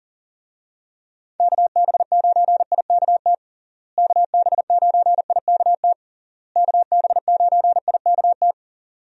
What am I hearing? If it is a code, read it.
Morse code, heard as a keyed tone, about 40 wpm, 700 Hz. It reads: KB9IKT KB9IKT KB9IKT
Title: USA Call Signs - 6 Characters - Repeated 3x - 40wpm